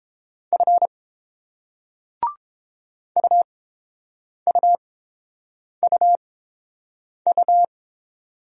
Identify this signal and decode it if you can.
Morse code, heard as a keyed tone, about 30 wpm, spaced as for 7 wpm, 700 Hz. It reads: F UUUU